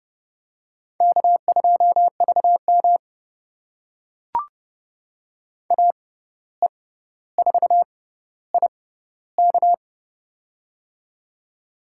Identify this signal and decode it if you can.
Morse code, heard as a keyed tone, about 30 wpm, spaced as for 14 wpm, 700 Hz. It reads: K2VM AE4IK
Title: USA Call Signs - 6x Character Spacing - 30wpm